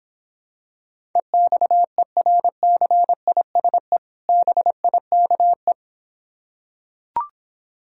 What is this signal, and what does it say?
Morse code, heard as a keyed tone, about 26 wpm, 700 Hz. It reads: EXERCISE BIKE